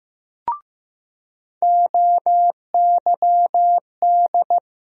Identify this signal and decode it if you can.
Morse code, heard as a keyed tone, about 15 wpm, 700 Hz. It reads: OYD
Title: USA Call Sign Postfixes - 3 Letters 15wpm